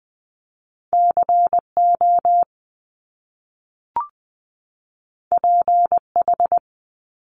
Morse code, heard as a keyed tone, about 20 wpm, 700 Hz. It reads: C O P H
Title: Top Two Letter Combinations in Words 20wpm